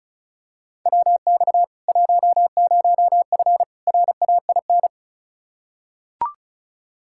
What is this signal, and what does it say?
Morse code, heard as a keyed tone, about 34 wpm, 700 Hz. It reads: WX 10F RAIN
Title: K1USN SST QSO Practice 35wpm